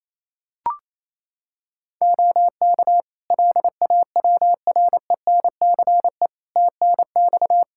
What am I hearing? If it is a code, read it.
Morse code, heard as a keyed tone, about 28 wpm, 700 Hz. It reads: OK LAWRENCE TNX
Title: Practice QSO - Exchange 4 - Ending - 28wpm